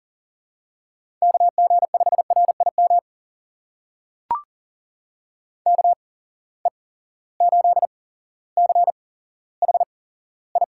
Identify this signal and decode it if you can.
Morse code, heard as a keyed tone, about 40 wpm, spaced as for 17 wpm, 700 Hz. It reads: KG5RIM KE8CHI
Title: USA Call Signs - 6 Characters - 8x Character Spacing - 40wpm